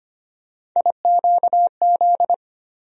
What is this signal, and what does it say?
Morse code, heard as a keyed tone, about 25 wpm, 700 Hz. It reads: IQZ
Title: USA Call Sign Postfixes - 3 Letters 25wpm